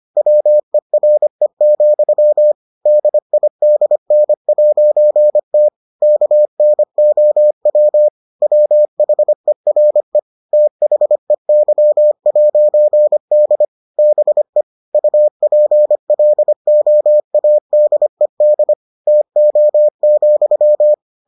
Morse code, heard as a keyed tone, about 25 wpm, 600 Hz. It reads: WERE, DIDN'T KNOW WHERE THEY'D BE UPLOADED TO,